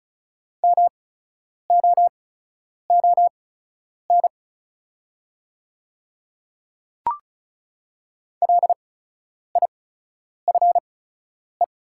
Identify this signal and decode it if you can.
Morse code, heard as a keyed tone, about 34 wpm, spaced as for 10 wpm, 690 Hz. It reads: MOON LIFE
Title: 8x Character Spacing - Top 500 Words - 35wpm